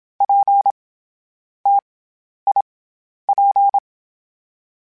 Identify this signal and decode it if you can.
Morse code, heard as a keyed tone, about 27 wpm, 800 Hz. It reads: P T I P
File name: Difficult Combos / Difficult Combos-mp3s / T AOEI @25